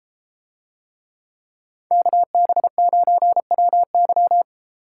Morse code, heard as a keyed tone, about 33 wpm, 700 Hz. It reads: KB9WY